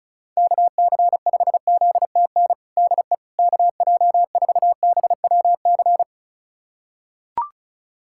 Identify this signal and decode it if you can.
Morse code, heard as a keyed tone, about 36 wpm, 700 Hz. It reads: KC5ZTN DE KJ4BWC
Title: USA Callsign DE Callsign 35wpm